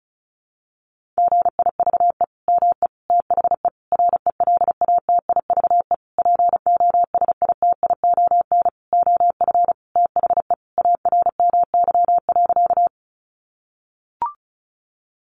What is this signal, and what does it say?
Morse code, heard as a keyed tone, about 34 wpm, 700 Hz. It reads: GIVE ME THE RELATIVE POSITION OF THE ARMY.
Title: Sentences from Top 1400 Words 35wpm